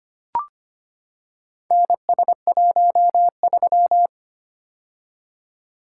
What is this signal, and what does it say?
Morse code, heard as a keyed tone, about 25 wpm, 700 Hz. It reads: NS13
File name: practice / 25wpm / Four Letter-Number Combinations - 25wpm